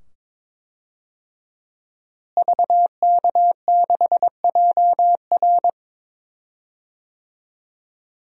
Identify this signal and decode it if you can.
Morse code, heard as a keyed tone, about 21 wpm, 700 Hz. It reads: VK6JR
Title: DXCC-100 Call Signs - 5 Characters - Repeated 2x - 22wpm